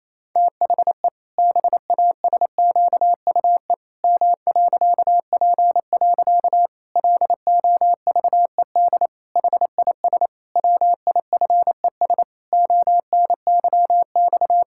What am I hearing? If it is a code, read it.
Morse code, heard as a keyed tone, about 28 wpm, 700 Hz. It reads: THE BASQUE M.P. LOVED HIS WIFES ONYX
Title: Pangram Challenge v2 - 28wpm